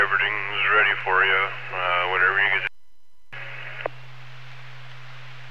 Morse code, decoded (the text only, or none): none